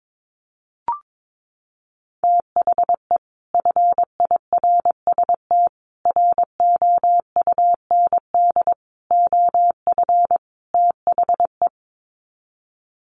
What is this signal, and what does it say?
Morse code, heard as a keyed tone, about 22 wpm, 700 Hz. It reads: THE FIRST ROUND OF THE
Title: Sets of 5 Words Encore 22wpm